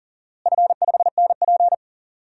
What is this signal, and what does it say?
Morse code, heard as a keyed tone, about 40 wpm, 700 Hz. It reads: F5NP
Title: DXCC-100 Call Signs - 4 Characters - Repeated 3x - 40wpm